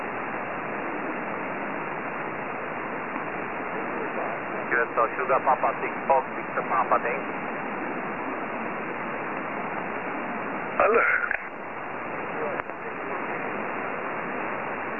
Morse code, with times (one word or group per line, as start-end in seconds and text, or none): none